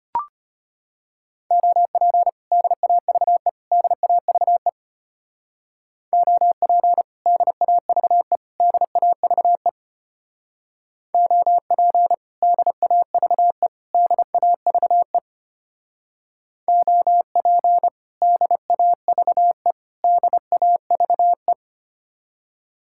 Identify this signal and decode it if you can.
Morse code, heard as a keyed tone, about 31 wpm, 700 Hz. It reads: OP DAVE DAVE OP DAVE DAVE OP DAVE DAVE OP DAVE DAVE
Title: Speed-Racing Common QSO Phrases - 25wpm